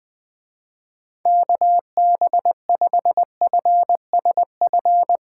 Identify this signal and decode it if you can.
Morse code, heard as a keyed tone, about 21 wpm, 700 Hz. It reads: KB5FSF
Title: Speed Racing - USA 2x3 Call Signs 17wpm